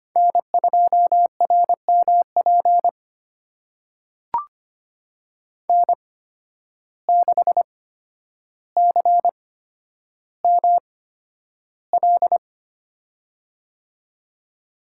Morse code, heard as a keyed tone, about 25 wpm, spaced as for 11 wpm, 700 Hz. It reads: N2RMP N6CML
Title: USA Call Signs - 8x Character Spacing - 25wpm